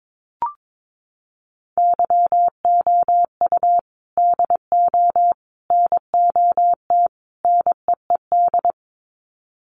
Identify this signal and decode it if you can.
Morse code, heard as a keyed tone, about 22 wpm, 700 Hz. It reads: YOU DO NOT NEED
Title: Sets of 4 Words 22wpm